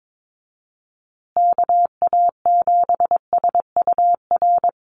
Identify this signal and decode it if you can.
Morse code, heard as a keyed tone, about 22 wpm, 700 Hz. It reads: KA7SUR
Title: USA Call Signs Repeated 2x 22wpm